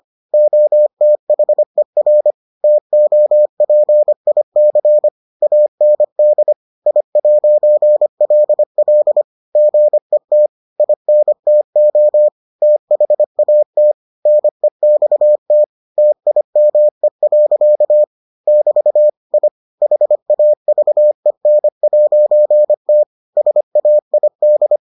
Morse code, heard as a keyed tone, about 25 wpm, 600 Hz. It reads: OTHER TOPIC AND I'LL GET INTO THAT NEXT TIME. <BT> I HAVEN'T SAID